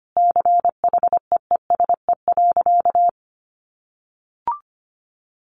Morse code, heard as a keyed tone, about 25 wpm, 700 Hz. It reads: CHEESE.
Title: Easy Phrases 25wpm